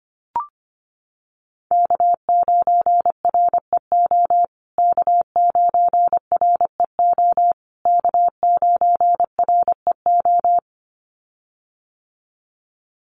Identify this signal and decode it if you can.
Morse code, heard as a keyed tone, about 25 wpm, 700 Hz. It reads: K9REO K9REO K9REO
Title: USA Call Signs Repeated 3x 25wpm